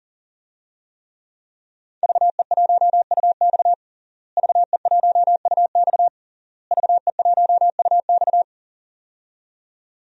Morse code, heard as a keyed tone, about 40 wpm, 700 Hz. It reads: VE1UX VE1UX VE1UX
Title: DXCC-100 Call Signs - 5 Characters - Repeated 3x - 40wpm